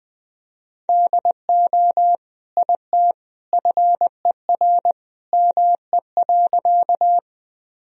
Morse code, heard as a keyed tone, about 20 wpm, 700 Hz. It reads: DO IT FER ME.
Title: Sentences from Top 200 Words and Abbreviations 20wpm